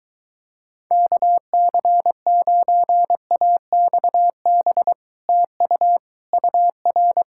Morse code, heard as a keyed tone, about 23 wpm, 700 Hz. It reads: KC9AXB TU UR